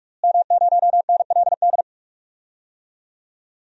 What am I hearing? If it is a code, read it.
Morse code, heard as a keyed tone, about 43 wpm, 700 Hz. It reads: M0NLD
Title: DXCC-100 Call Signs - 5 Characters - Repeated 2x - 45wpm